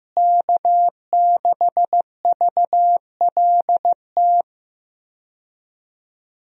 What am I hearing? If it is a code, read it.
Morse code, heard as a keyed tone, about 15 wpm, 700 Hz. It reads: K6VLT
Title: USA Call Signs - 5 Characters - Repeated 3x - 15wpm